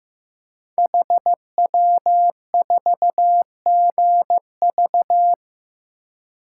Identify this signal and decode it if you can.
Morse code, heard as a keyed tone, about 15 wpm, 700 Hz. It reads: HW4GV
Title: Code Groups - 5 Letter:Numbers - 15wpm